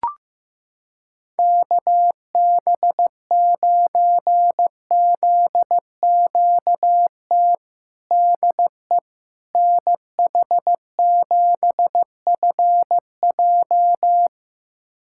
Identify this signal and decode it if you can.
Morse code, heard as a keyed tone, about 15 wpm, 700 Hz. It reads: KB9ZQT DE NH7FJ
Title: USA Callsign DE Callsign 15wpm